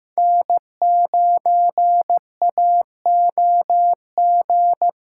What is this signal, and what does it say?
Morse code, heard as a keyed tone, about 15 wpm, 700 Hz. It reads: N9AOG
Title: USA Callsign DE Callsign - 5 Characters - 15wpm